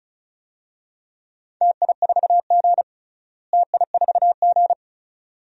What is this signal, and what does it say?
Morse code, heard as a keyed tone, about 36 wpm, 700 Hz. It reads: TI4G TI4G
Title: DXCC-100 Call Signs - 4 Characters - Repeated 2x - 35wpm